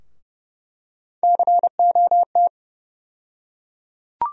Morse code, heard as a keyed tone, about 31 wpm, 700 Hz. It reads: COT
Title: USA Call Sign Postfixes - 3 Letters 30wpm